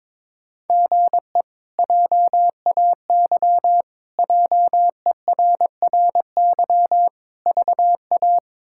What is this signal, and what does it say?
Morse code, heard as a keyed tone, about 22 wpm, 700 Hz. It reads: GE JAY JERRY VA
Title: K1USN SST QSO Practice - 22wpm